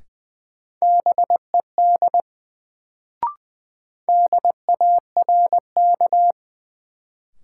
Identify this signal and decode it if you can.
Morse code, heard as a keyed tone, about 20 wpm, 700 Hz. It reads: BED DARK
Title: Sentences from Top 500 Words - New Word Drill - 20wpm